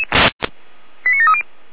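Five-tone selective call (ZVEI), heard as quick runs of five tones, tones of about 70 ms. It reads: 89730
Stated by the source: ZVEI five-tone call received by SDR, from feuerwehr-orlamuende.org audio archive.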